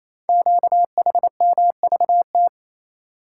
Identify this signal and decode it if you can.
Morse code, heard as a keyed tone, about 29 wpm, 700 Hz. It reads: QHMVT
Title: Code Groups - 5 Letters - 28wpm